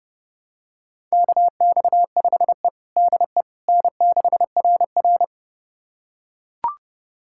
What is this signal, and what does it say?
Morse code, heard as a keyed tone, about 32 wpm, 700 Hz. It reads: KX5E DE N6RR
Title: USA Callsign DE Callsign - 4 Characters - 30wpm